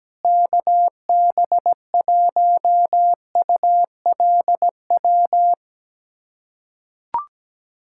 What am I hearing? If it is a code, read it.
Morse code, heard as a keyed tone, about 17 wpm, 700 Hz. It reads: KB1ULW
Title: USA Callsign DE Callsign 17wpm